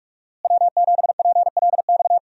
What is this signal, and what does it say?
Morse code, heard as a keyed tone, about 45 wpm, 700 Hz. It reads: W7PLX
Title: USA Call Signs - 5 Characters - Repeated 3x - 45wpm